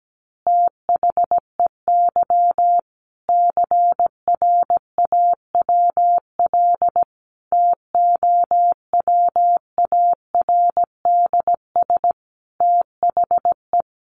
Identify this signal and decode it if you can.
Morse code, heard as a keyed tone, about 17 wpm, 700 Hz. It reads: THEY CRAWL TOWARDS THE
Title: Additional Easy Phrases 17wpm